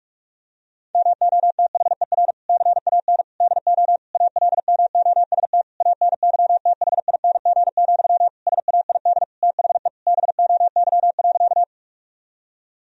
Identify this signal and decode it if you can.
Morse code, heard as a keyed tone, about 46 wpm, 700 Hz. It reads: MOTHER CAN DO ALMOST ANYTHING, SAID THE BOY.
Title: Sentences from Top 1000 Words - Review Part 4 - 45wpm